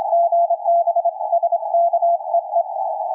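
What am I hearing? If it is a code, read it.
Morse code, heard as a keyed tone, about 27 wpm, 680 Hz, with radio noise. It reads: GB SKEE
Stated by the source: QRP station's CW heard around 7029.2 kHz; signal fading from a solid 559 down to complete dropout.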